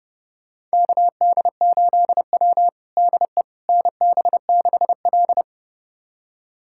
Morse code, heard as a keyed tone, about 30 wpm, 700 Hz. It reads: KD8W DE NB6L
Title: USA Callsign DE Callsign - 4 Characters - 30wpm